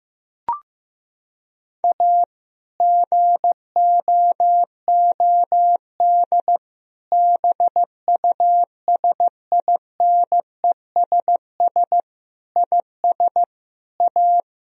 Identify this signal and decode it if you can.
Morse code, heard as a keyed tone, about 15 wpm, 700 Hz. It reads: A GOOD BUSINESS IS A
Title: Sentences from Top 1000 Words - Review Part 4 - 15wpm